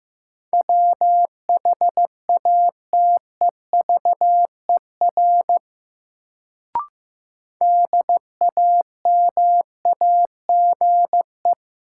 Morse code, heard as a keyed tone, about 15 wpm, 700 Hz. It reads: WHATEVER DAMAGE
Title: Sentences from Top 1100 Words - New Word Drill - 15wpm